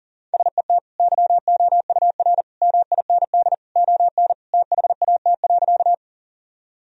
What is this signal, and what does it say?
Morse code, heard as a keyed tone, about 40 wpm, 700 Hz. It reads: SET YOUR MIND ON THAT.
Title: Sentences from Top 1000 Words - Review Part 1 - 40wpm